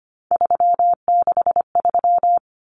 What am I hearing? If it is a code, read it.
Morse code, heard as a keyed tone, about 25 wpm, 700 Hz. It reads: 363
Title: Common Cut Numbers - 3-Digit 25wpm